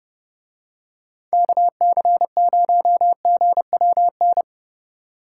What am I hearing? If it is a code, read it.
Morse code, heard as a keyed tone, about 31 wpm, 700 Hz. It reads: KC0GWN